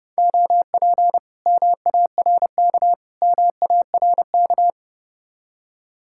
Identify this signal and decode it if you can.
Morse code, heard as a keyed tone, about 32 wpm, 700 Hz. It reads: OP MARK MARK